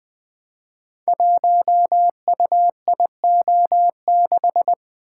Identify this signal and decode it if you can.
Morse code, heard as a keyed tone, about 20 wpm, 700 Hz. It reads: 1UIO6
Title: Code Groups - 5 Letter:Numbers - 20wpm